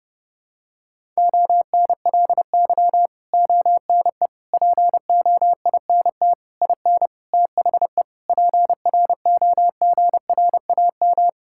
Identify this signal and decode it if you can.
Morse code, heard as a keyed tone, about 32 wpm, 700 Hz. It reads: ONLY ONE POINT IN THE PROGRAM
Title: Sentences from Top 1000 Words - Review Part 5 - 30wpm